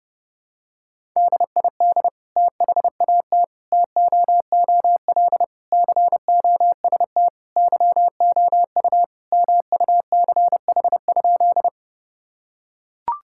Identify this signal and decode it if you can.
Morse code, heard as a keyed tone, about 28 wpm, 700 Hz. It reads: DID THAT TOOL COST YOU MUCH?
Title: Sentences from Top 1000 Words - Review Part 6 - 30wpm